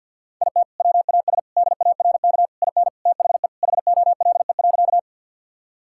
Morse code, heard as a keyed tone, about 51 wpm, 700 Hz. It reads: IT WAS DARK IN THE HOLE.
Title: Sentences from Top 1600 Words 50wpm